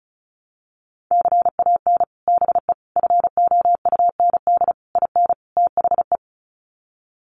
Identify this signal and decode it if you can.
Morse code, heard as a keyed tone, about 34 wpm, 700 Hz. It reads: CAN BE FOUND IN THE
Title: Sets of 5 Words Encore 35wpm